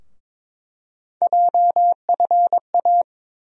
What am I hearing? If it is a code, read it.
Morse code, heard as a keyed tone, about 21 wpm, 700 Hz. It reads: JFA
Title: USA Call Sign Postfixes - 3 Letters 22wpm